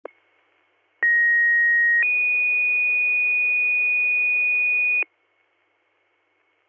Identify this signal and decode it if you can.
Two-tone page, heard as a one-second tone, then a three-second tone, about 1900 then 2450 Hz. It is heard from 1.0 to 5.0 s.